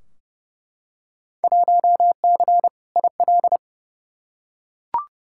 Morse code, heard as a keyed tone, about 30 wpm, 700 Hz. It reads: 1C IL
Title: Field Day Exchanges - 30wpm